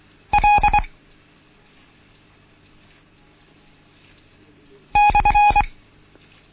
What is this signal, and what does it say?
Morse code, heard as a keyed tone, about 24 wpm, 830 Hz. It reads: L /
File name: Morsen / Kurs A 16